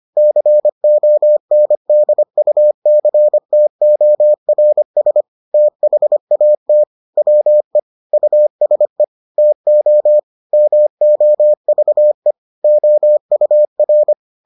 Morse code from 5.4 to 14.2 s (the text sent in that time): THAT WE USE TO MOVE OUR